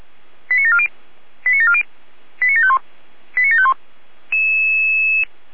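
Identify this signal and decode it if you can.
Five-tone selective call (ZVEI), heard as quick runs of five tones, tones of about 70 ms. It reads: 89740 89740 89751 89751
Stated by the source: ZVEI five-tone call 89590 received by SDR.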